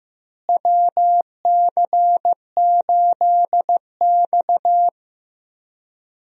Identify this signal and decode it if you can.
Morse code, heard as a keyed tone, about 15 wpm, 700 Hz. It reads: WC8X
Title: USA Call Signs - 4 Characters - Repeated 3x - 15wpm